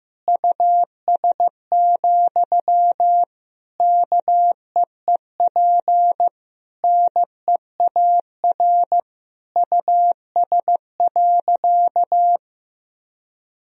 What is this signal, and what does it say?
Morse code, heard as a keyed tone, about 15 wpm, 700 Hz. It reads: US, KEEP NEAR US.
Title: Sentences from Top 1000 Words - Review Part 2 - 15wpm